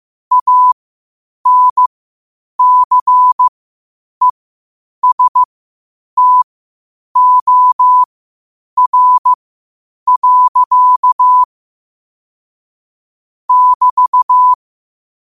Morse code, heard as a keyed tone, about 15 wpm, spaced as for 8 wpm, 1000 Hz. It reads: ANCESTOR. <BT>
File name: qotdfiles / 8wpm